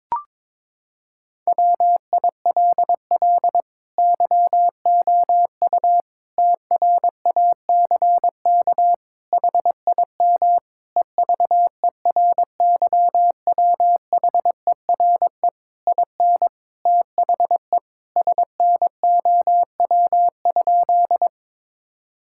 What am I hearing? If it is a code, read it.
Morse code, heard as a keyed tone, about 22 wpm, 700 Hz. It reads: WILL YOU TRACK HIM EVERYWHERE IN THE SNOW?
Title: Sentences from Top 1600 Words 22wpm